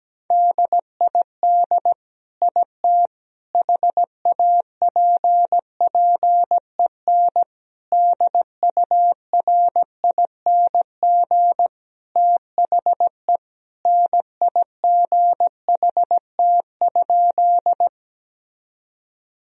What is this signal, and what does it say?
Morse code, heard as a keyed tone, about 17 wpm, 700 Hz. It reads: DID IT HAPPEN DURING THE NIGHT?
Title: Sentences from Top 1000 Words - Review Part 2 - 17wpm